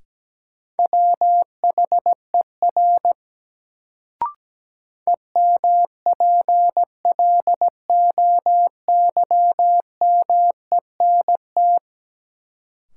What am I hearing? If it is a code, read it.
Morse code, heard as a keyed tone, about 17 wpm, 700 Hz. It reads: WHER EMPLOYMENT
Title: Sentences from Top 1500 Words - New Word Drill Alt - 17wpm